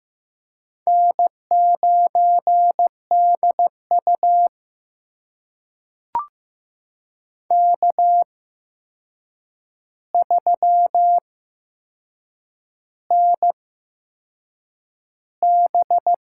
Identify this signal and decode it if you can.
Morse code, heard as a keyed tone, about 15 wpm, spaced as for 6 wpm, 700 Hz. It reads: N9DU K3NB
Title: USA Call Signs - 4 Characters - 8x Character Spacing - 15wpm